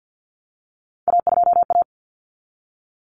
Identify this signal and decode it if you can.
Morse code, heard as a keyed tone, about 48 wpm, 700 Hz. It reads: A2A
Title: Common Cut Numbers - 3-Digit 50wpm